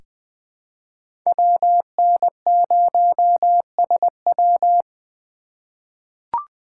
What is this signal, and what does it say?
Morse code, heard as a keyed tone, about 20 wpm, 700 Hz. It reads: WN0SW
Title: USA Call Signs - 5 Characters - 2x Character Spacing - 20wpm